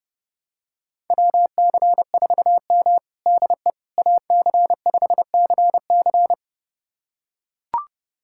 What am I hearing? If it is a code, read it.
Morse code, heard as a keyed tone, about 31 wpm, 700 Hz. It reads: WC4M DE AC5CC
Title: USA Callsign DE Callsign 30wpm